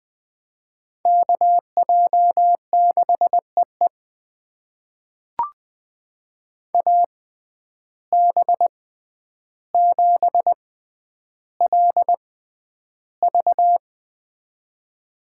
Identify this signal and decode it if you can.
Morse code, heard as a keyed tone, about 20 wpm, spaced as for 10 wpm, 700 Hz. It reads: KJ6EE AB7LV
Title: USA Call Signs - 5 Characters - 6x Character Spacing - 20wpm